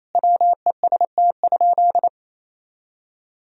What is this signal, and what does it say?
Morse code, heard as a keyed tone, about 28 wpm, 700 Hz. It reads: WEST?
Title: Sentences from Top 1000 Words - Review Part 2 - 28wpm